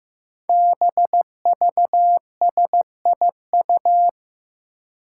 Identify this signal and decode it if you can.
Morse code, heard as a keyed tone, about 15 wpm, 700 Hz. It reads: BVSIU
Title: Five Challenging Letter-Number Combinations 15wpm